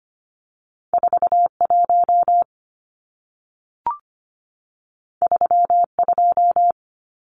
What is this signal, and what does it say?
Morse code, heard as a keyed tone, about 25 wpm, 700 Hz. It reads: 41 32
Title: Numbers - 2-Digit 25wpm